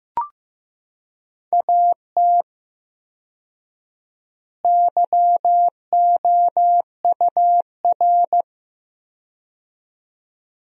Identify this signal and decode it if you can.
Morse code, heard as a keyed tone, about 15 wpm, 700 Hz. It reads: AT YOUR
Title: 4x Word-Spacing - Sentences from Top 200 Words - 15wpm